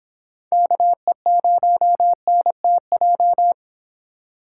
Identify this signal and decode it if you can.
Morse code, heard as a keyed tone, about 25 wpm, 700 Hz. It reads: KE0NTJ